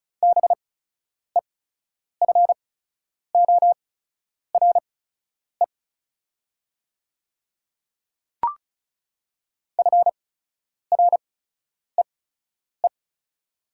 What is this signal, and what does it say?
Morse code, heard as a keyed tone, about 36 wpm, spaced as for 10 wpm, 700 Hz. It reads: BEFORE FREE
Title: 8x Character Spacing - Top 500 Words - 35wpm